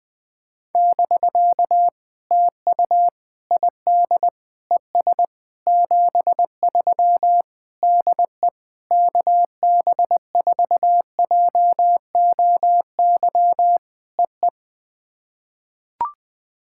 Morse code, heard as a keyed tone, about 20 wpm, 700 Hz. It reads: <BK> TU ID ES 73 DE KB4JOY EE
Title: POTA QSO Practice - 20wpm